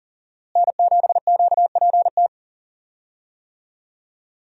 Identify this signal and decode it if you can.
Morse code, heard as a keyed tone, about 40 wpm, 700 Hz. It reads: N7QPT
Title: USA Call Signs - 5 Characters - Repeated 3x - 40wpm